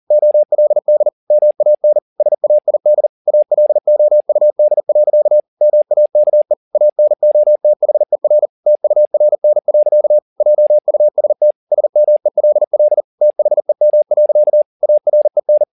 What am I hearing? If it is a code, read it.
Morse code, heard as a keyed tone, about 40 wpm, 600 Hz. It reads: OLD MAN SAID ALOUD. MAKE ANOTHER TURN. JUST SMELL THEM. AREN